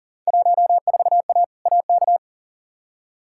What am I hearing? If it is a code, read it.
Morse code, heard as a keyed tone, about 37 wpm, 700 Hz. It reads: 14A AK
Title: Field Day Exchanges Encore - 40wpm